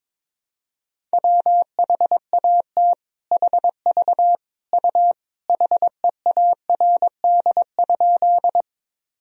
Morse code, heard as a keyed tone, about 22 wpm, 700 Hz. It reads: WHAT HV U HEARD?
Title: Sentences from Top 400 Words and Abbreviations 22wpm